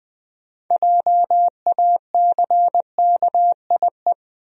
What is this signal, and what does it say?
Morse code, heard as a keyed tone, about 20 wpm, 700 Hz. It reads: JACKIE
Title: USA Common Names - 20wpm